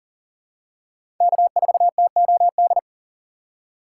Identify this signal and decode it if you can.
Morse code, heard as a keyed tone, about 40 wpm, 700 Hz. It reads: K4TOD
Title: USA Call Signs - 5 Characters - 4x Character Spacing - 40wpm